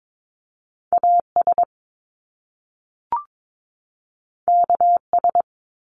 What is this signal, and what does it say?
Morse code, heard as a keyed tone, about 22 wpm, 700 Hz. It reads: AS KS
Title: Top Two Letter Combinations in Words 22wpm